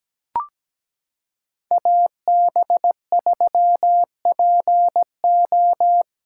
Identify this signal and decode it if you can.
Morse code, heard as a keyed tone, about 17 wpm, 700 Hz. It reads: AB3PO
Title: USA Callsign DE Callsign - 5 Characters - 17wpm